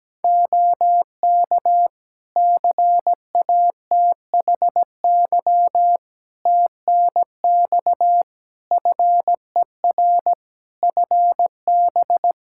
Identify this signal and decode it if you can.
Morse code, heard as a keyed tone, about 17 wpm, 700 Hz. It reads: OK CATHY TNX FER FB
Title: K1USN SST QSO Practice 17wpm